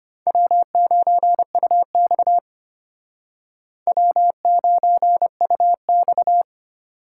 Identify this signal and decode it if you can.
Morse code, heard as a keyed tone, about 35 wpm, 700 Hz. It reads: W9UX W9UX